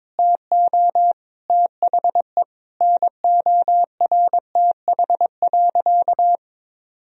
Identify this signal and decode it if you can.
Morse code, heard as a keyed tone, about 22 wpm, 700 Hz. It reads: TO THE NORTH.
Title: Sentences from Top 1000 Words - Review Part 1 - 22wpm